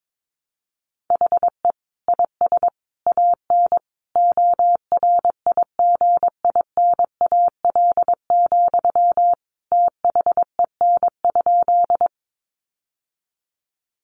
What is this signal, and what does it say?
Morse code, heard as a keyed tone, about 22 wpm, 700 Hz. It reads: HE IS AN ORIGINAL, THEN?
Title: Sentences from Top 900 Words 22wpm